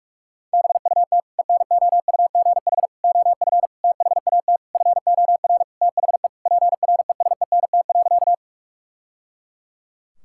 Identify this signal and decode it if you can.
Morse code, heard as a keyed tone, about 46 wpm, 700 Hz. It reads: BUT ENOUGH OF THAT FOR THE PRESENT.
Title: Sentences from Top 1000 Words - Review Part 5 - 45wpm